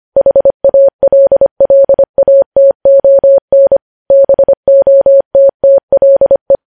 Morse code, heard as a keyed tone, about 25 wpm, 550 Hz. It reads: HALLATON BOTTLE
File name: podcast / News headlines-1x-25x25